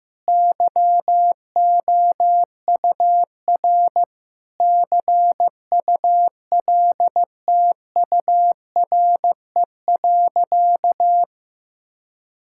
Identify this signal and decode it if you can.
Morse code, heard as a keyed tone, about 15 wpm, 700 Hz. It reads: YOUR CULTURE.
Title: Sentences from Top 1000 Words - Review Part 6 - 15wpm